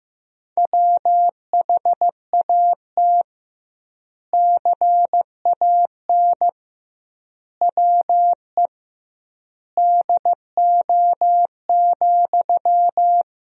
Morse code, heard as a keyed tone, about 15 wpm, 700 Hz. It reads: WHAT CAN WE DO,
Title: 2x Word-Spacing - Sentences from Top 100 Words - 15wpm